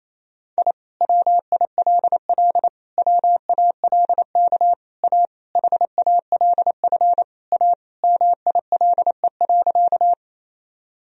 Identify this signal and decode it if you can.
Morse code, heard as a keyed tone, about 27 wpm, 700 Hz. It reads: I WILL WALK A HALF A MILE.
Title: Sentences from Top 1000 Words - Review Part 1 - 28wpm